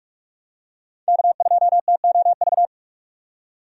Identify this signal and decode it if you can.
Morse code, heard as a keyed tone, about 47 wpm, 700 Hz. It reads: K2TOV
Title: USA Call Signs - 5 Characters - 4x Character Spacing - 45wpm